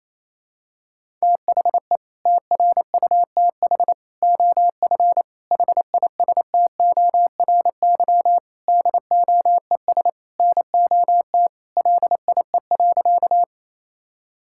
Morse code, heard as a keyed tone, about 28 wpm, 700 Hz. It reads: THE TRUTH OF HISTORY DOES NOT LIE.
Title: Sentences from Top 1600 Words 28wpm